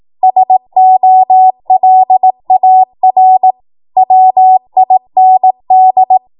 Morse code, heard as a keyed tone, about 18 wpm, 750 Hz. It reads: SOLAR WIND
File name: Morse / 18WPM